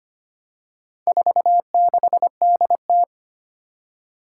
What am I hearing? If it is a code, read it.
Morse code, heard as a keyed tone, about 25 wpm, 700 Hz. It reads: 46DT
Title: Four Challenging Letter-Number Combinations 25wpm